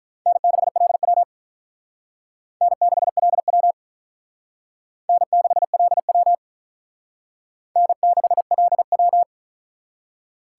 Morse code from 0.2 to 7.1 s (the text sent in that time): N6LW N6LW N6LW